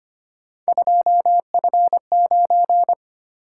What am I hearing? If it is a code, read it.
Morse code, heard as a keyed tone, about 25 wpm, 700 Hz. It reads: 2F9
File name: Three Letter-Number Combinations - 25wpm